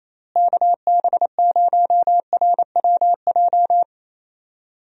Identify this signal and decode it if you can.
Morse code, heard as a keyed tone, about 28 wpm, 700 Hz. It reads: KB0RWJ
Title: USA Call Signs - 6 Characters - Repeated 2x - 28wpm